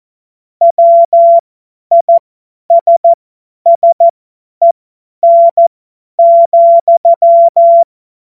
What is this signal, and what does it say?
Morse code, heard as a keyed tone, about 14 wpm, spaced as for 10 wpm, 680 Hz. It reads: WISSEN,